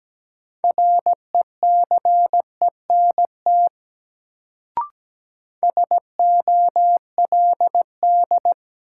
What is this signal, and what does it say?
Morse code, heard as a keyed tone, about 17 wpm, 700 Hz. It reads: RECENT SOLD
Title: Sentences from Top 1400 Words - New Word Drill Alt - 17wpm